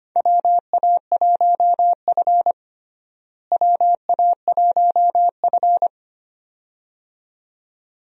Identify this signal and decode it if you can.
Morse code, heard as a keyed tone, about 25 wpm, 700 Hz. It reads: WA1F WA1F
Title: USA Call Signs - 4 Characters - Repeated 2x - 25wpm